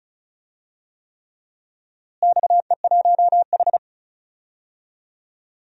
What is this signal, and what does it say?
Morse code, heard as a keyed tone, about 36 wpm, 690 Hz. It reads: XE1H
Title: DXCC-100 Call Signs - 4 Characters - Repeated 3x - 35wpm